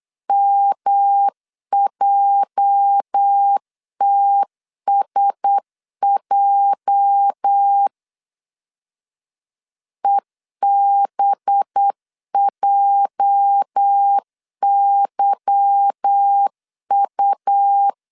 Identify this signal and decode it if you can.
Morse code, heard as a keyed tone, about 8 wpm, 800 Hz. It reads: MJTSJ EBJYU
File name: LEZIONI / 8 YJ45 / 8 YJ 8wpm 3313